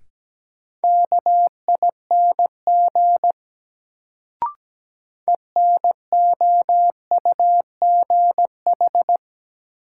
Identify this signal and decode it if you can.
Morse code, heard as a keyed tone, about 17 wpm, 700 Hz. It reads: KING ENOUGH
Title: Sentences from Top 400 Words - New Word Drill - 17wpm